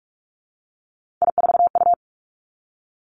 Morse code, heard as a keyed tone, about 46 wpm, 700 Hz. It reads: I4U
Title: Three Challenging Letter-Number Combinations 45wpm